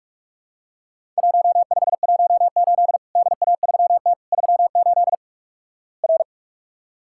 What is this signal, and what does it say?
Morse code, heard as a keyed tone, about 48 wpm, 700 Hz. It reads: 1518 DA3T 38 R
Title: Worked All Europe DX Contest - QTC Exchange - 45wpm